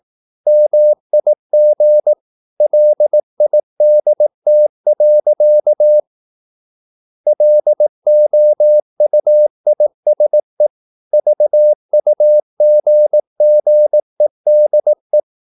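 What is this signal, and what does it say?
Morse code, heard as a keyed tone, about 18 wpm, 600 Hz. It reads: MIG LIDT. LOUISE VUGGEDE
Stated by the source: Slow (QRS) Morse practice at 18 wpm, CW med Gnister.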